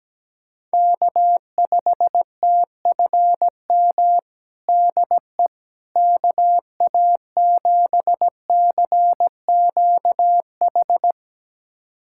Morse code, heard as a keyed tone, about 17 wpm, 700 Hz. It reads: K5TFM DE KA7CQH
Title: USA Callsign DE Callsign 17wpm